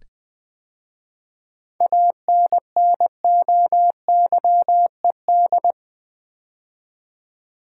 Morse code, heard as a keyed tone, about 20 wpm, 700 Hz. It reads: ANNOYED
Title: 7 Letter Words - First 3 Letters Given - 20wpm